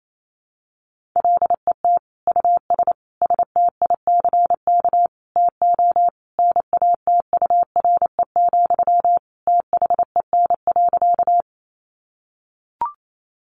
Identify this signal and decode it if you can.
Morse code, heard as a keyed tone, about 28 wpm, 700 Hz. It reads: LET US STICK TO NATURE, THEN.
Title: Sentences from Top 1200 Words 28wpm